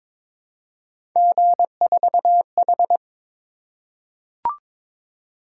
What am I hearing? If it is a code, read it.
Morse code, heard as a keyed tone, about 22 wpm, 700 Hz. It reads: G4H
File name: practice / 22wpm / Three Letter-Number Combinations - 22wpm